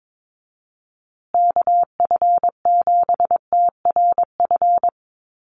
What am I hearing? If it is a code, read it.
Morse code, heard as a keyed tone, about 22 wpm, 700 Hz. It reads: KF7TRF
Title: USA Call Signs Repeated 2x 22wpm